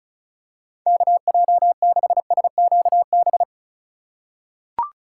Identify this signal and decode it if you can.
Morse code, heard as a keyed tone, about 36 wpm, 700 Hz. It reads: KJ6SQB
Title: USA Call Signs - 6 Characters - 3x Character Spacing - 35wpm